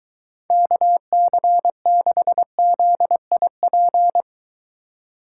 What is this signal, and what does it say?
Morse code, heard as a keyed tone, about 23 wpm, 700 Hz. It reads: KC6ZIP